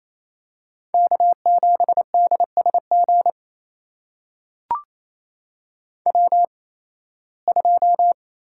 Morse code, heard as a keyed tone, about 28 wpm, spaced as for 18 wpm, 700 Hz. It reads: K7DSG W2
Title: USA Call Signs - 5 Characters - 8x Character Spacing - 28wpm